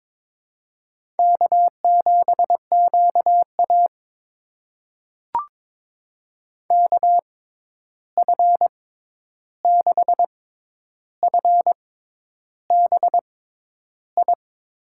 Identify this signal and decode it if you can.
Morse code, heard as a keyed tone, about 22 wpm, spaced as for 10 wpm, 700 Hz. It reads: K7QA KF6FBI
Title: USA Call Signs - 6x Character Spacing - 22wpm